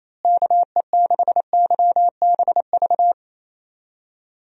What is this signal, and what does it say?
Morse code, heard as a keyed tone, about 27 wpm, 700 Hz. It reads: KE6YBV
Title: USA Call Signs - 6 Characters - Repeated 3x - 28wpm